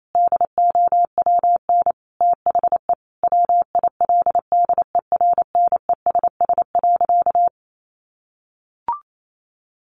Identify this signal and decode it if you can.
Morse code, heard as a keyed tone, about 28 wpm, 700 Hz. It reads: DOWN THE WILDERNESS.
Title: A Sound Of Thunder 28wpm